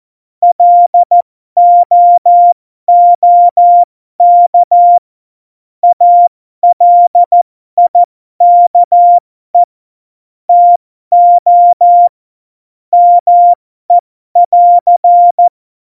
Morse code, heard as a keyed tone, about 14 wpm, spaced as for 12 wpm, 700 Hz. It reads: LOOK ALIKE TO ME <AR>
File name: Morsekurs / Witze